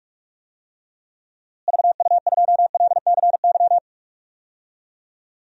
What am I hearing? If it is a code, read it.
Morse code, heard as a keyed tone, about 46 wpm, 700 Hz. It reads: VU2LCY
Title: DXCC-100 Call Signs - 6 Characters - Repeated 2x - 45wpm